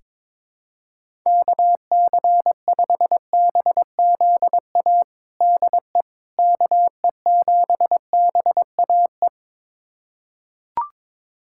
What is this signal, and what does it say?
Morse code, heard as a keyed tone, about 22 wpm, 700 Hz. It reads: KC5BZA DE KE7BAE
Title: USA Callsign DE Callsign 22wpm